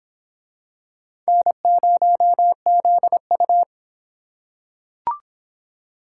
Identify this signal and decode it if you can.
Morse code, heard as a keyed tone, about 26 wpm, 700 Hz. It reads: N0ZU